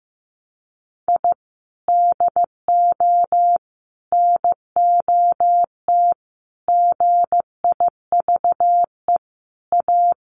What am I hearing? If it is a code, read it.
Morse code, heard as a keyed tone, about 15 wpm, 700 Hz. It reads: I DO NOT GIVE A
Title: Sets of 5 Words Encore2 15wpm